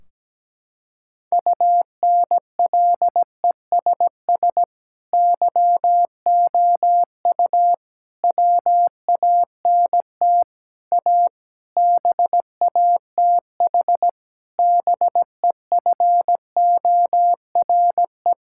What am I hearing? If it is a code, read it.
Morse code, heard as a keyed tone, about 17 wpm, 700 Hz. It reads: UNLESS YOU WANT A BATH BEFORE
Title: Sentences from Top 1600 Words 17wpm